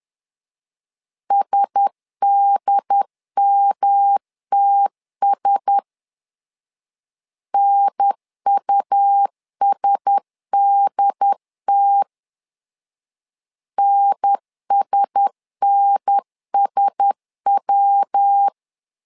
Morse code, heard as a keyed tone, about 10 wpm, 800 Hz. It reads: SDMTS NUSDT NSNSW
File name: LEZIONI / 4 Q1S 10wpm 2133